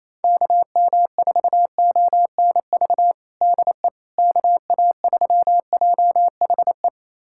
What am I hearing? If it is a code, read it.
Morse code, heard as a keyed tone, about 27 wpm, 700 Hz. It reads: KM4ONV DE KA3JHE
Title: USA Callsign DE Callsign - 6 Characters - 28wpm